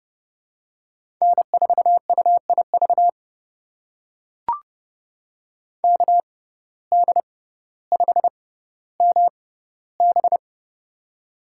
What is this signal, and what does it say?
Morse code, heard as a keyed tone, about 30 wpm, spaced as for 15 wpm, 700 Hz. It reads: N4UIV KD5MB
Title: USA Call Signs - 5 Characters - 6x Character Spacing - 30wpm